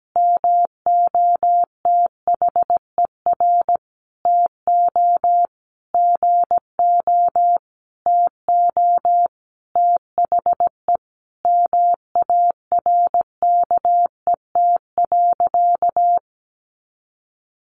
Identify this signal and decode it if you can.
Morse code, heard as a keyed tone, about 17 wpm, 700 Hz. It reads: MOTHER TO GO TO THE MARKET.
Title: Sentences from Top 1400 Words 17wpm